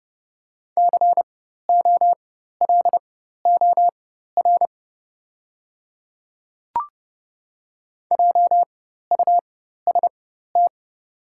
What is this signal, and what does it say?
Morse code, heard as a keyed tone, about 30 wpm, spaced as for 14 wpm, 700 Hz. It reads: COLOR JUST
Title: 4x Character Spacing - Top 300 Words - 30wpm